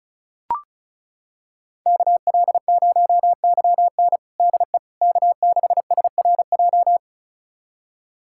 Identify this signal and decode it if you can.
Morse code, heard as a keyed tone, about 36 wpm, 700 Hz. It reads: KL0YN DE K6SRJ
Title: USA Callsign DE Callsign - 5 Characters - 35wpm